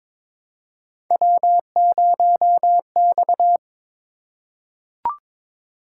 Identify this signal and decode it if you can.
Morse code, heard as a keyed tone, about 23 wpm, 700 Hz. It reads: W0X
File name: practice / 22wpm / Three Letter-Number Combinations - 22wpm